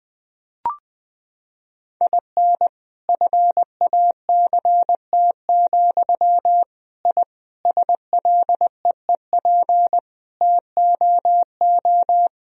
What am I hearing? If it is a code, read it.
Morse code, heard as a keyed tone, about 20 wpm, 700 Hz. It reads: IN FACT, I SLEEP TOO
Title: Sentences from Top 1000 Words - Review Part 3 - 20wpm